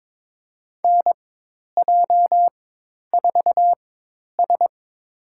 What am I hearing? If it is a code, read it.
Morse code, heard as a keyed tone, about 23 wpm, spaced as for 10 wpm, 700 Hz. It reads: NJ4S
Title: USA Call Signs - 4 Characters - 4x Character Spacing - 22wpm